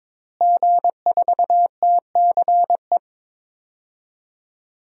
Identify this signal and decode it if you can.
Morse code, heard as a keyed tone, about 22 wpm, 700 Hz. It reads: G4TCE
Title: DXCC-100 Call Signs - 5 Characters - Repeated 3x - 22wpm